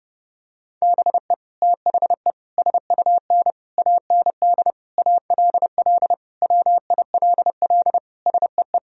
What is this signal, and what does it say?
Morse code, heard as a keyed tone, about 30 wpm, 700 Hz. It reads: BE THE SUN AND ALL WILL SEE
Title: Sentences from Top 1000 Words - Review Part 1 - 30wpm